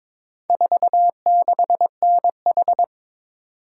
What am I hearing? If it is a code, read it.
Morse code, heard as a keyed tone, about 22 wpm, 700 Hz. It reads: 46NH
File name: practice / 22wpm / Four Letter-Number Combinations - 22wpm